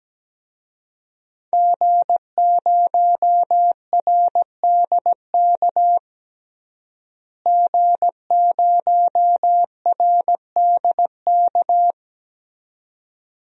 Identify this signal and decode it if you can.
Morse code, heard as a keyed tone, about 17 wpm, 700 Hz. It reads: G0RDK G0RDK
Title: DXCC-100 Call Signs - 5 Characters - Repeated 2x - 17wpm